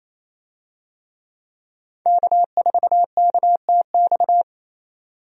Morse code, heard as a keyed tone, about 28 wpm, 700 Hz. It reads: K4KTX
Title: USA Call Signs - 5 Characters - 28wpm